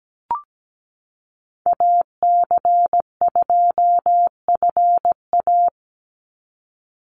Morse code, heard as a keyed tone, about 17 wpm, 700 Hz. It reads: AC2FA
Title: Call Signs - 17wpm